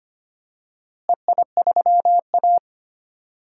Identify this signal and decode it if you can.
Morse code, heard as a keyed tone, about 25 wpm, 700 Hz. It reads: EI3A
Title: DXCC-100 Call Signs - 4 Characters - Repeated 2x - 25wpm